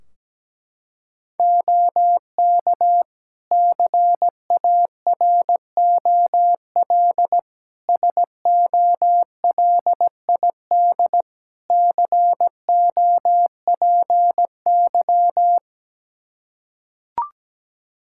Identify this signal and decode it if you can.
Morse code, heard as a keyed tone, about 17 wpm, 700 Hz. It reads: OK CAROL SOLID COPY
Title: K1USN SST QSO Practice 17wpm